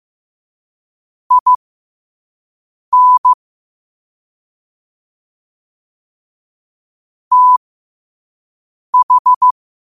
Morse code, heard as a keyed tone, about 15 wpm, spaced as for 5 wpm, 1000 Hz. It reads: IN TH